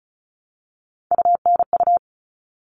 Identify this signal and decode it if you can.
Morse code, heard as a keyed tone, about 34 wpm, 700 Hz. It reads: UNU
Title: All Cut Numbers - 3-Digit 35wpm